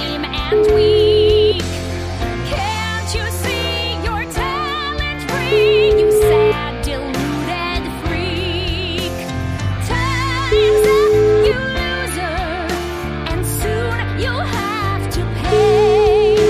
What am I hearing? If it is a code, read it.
Ringback tone (Europe, 425 Hz), with noise.